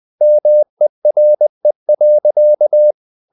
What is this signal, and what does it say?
Morse code, heard as a keyed tone, about 20 wpm, 600 Hz. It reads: MERE.
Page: CW med Gnister